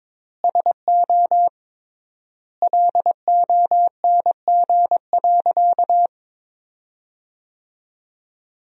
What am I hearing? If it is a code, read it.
Morse code, heard as a keyed tone, about 21 wpm, 700 Hz. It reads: SO LONG.
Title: 3x Word-Spacing - Sentences from Top 100 Words - 22wpm